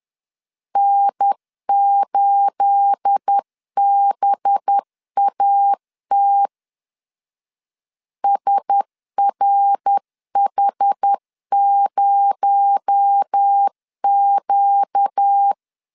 Morse code, heard as a keyed tone, about 10 wpm, 800 Hz. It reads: N8BAT SRH0Q
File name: LEZIONI / 9 286 10wpm 2735